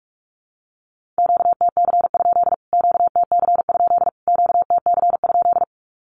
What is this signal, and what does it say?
Morse code, heard as a keyed tone, about 44 wpm, 700 Hz. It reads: QTC? QTC? QTC?
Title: Q-Codes Repeated 3x 45wpm